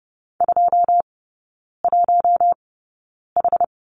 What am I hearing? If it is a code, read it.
Morse code, heard as a keyed tone, about 30 wpm, spaced as for 9 wpm, 700 Hz. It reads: 21H
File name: 081 Lesson 14e - Number 2 - Three Characters 30wpm